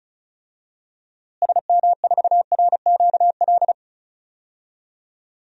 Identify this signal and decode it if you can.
Morse code, heard as a keyed tone, about 36 wpm, 700 Hz. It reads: SM4RQL
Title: DXCC-100 Call Signs - 6 Characters - Repeated 2x - 35wpm